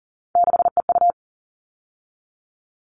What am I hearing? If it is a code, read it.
Morse code, heard as a keyed tone, about 40 wpm, 700 Hz. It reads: 6EU
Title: All Cut Numbers - 3-Digit 40wpm